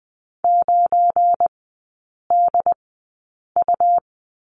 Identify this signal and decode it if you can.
Morse code, heard as a keyed tone, about 20 wpm, spaced as for 8 wpm, 700 Hz. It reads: 9DU
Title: 184 Lesson 30e - Letter b - Three Characters 20wpm